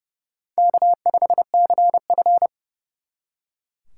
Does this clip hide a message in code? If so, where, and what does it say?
Yes: Morse code from 0.6 to 2.5 s, heard as a keyed tone, about 39 wpm, 700 Hz. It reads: K5CF